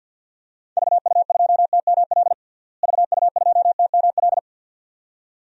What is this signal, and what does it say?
Morse code, heard as a keyed tone, about 46 wpm, 700 Hz. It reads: VU2TML VU2TML
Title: DXCC-100 Call Signs - 6 Characters - Repeated 2x - 50wpm